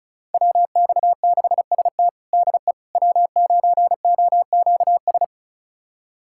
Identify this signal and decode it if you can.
Morse code, heard as a keyed tone, about 36 wpm, 700 Hz. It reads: WX6ST DE W9OQS
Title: USA Callsign DE Callsign - 5 Characters - 35wpm